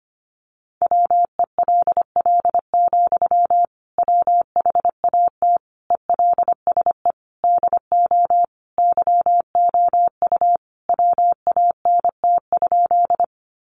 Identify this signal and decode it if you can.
Morse code, heard as a keyed tone, about 25 wpm, 700 Hz. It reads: WELL, WHAT ELSE DO YOU WANT?
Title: Sentences from Top 800 Words 25wpm